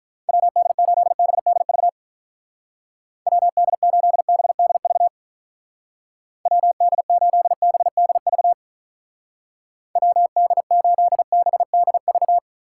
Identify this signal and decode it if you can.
Morse code, heard as a keyed tone, about 44 wpm, 700 Hz. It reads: WD8BDV WD8BDV WD8BDV WD8BDV